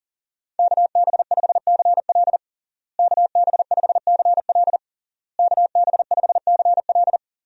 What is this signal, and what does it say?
Morse code, heard as a keyed tone, about 37 wpm, 700 Hz. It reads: KB5CL KB5CL KB5CL
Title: USA Call Signs - 5 Characters - Repeated 3x - 40wpm